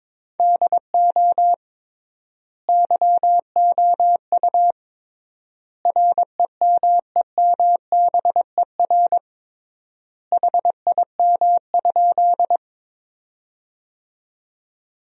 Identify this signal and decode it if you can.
Morse code, heard as a keyed tone, about 22 wpm, 700 Hz. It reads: DO YOU REMEMBER HIM?
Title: 3x Word-Spacing - Sentences from Top 400 Words - 22wpm